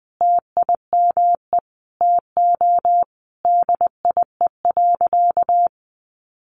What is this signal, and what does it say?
Morse code, heard as a keyed tone, about 20 wpm, 700 Hz. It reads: TIME TO DIE.
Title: Sentences from Top 1300 Words 20wpm